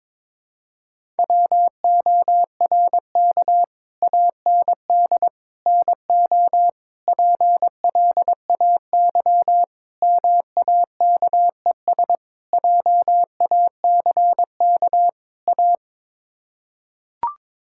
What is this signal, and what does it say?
Morse code, heard as a keyed tone, about 22 wpm, 700 Hz. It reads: WORK AND NO PLAY MAKES JACK A
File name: practice / 22wpm / Sets of 7 Words - 22wpm